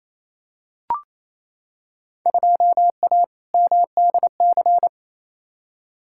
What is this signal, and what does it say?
Morse code, heard as a keyed tone, about 28 wpm, 700 Hz. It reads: 2A MDC
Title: Field Day Exchanges Encore - 28wpm